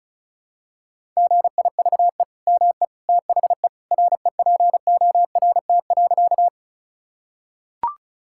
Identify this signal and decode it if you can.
Morse code, heard as a keyed tone, about 34 wpm, 700 Hz. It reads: GIVE ME THE REPORT.
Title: Sentences from Top 1000 Words - Review Part 6 - 35wpm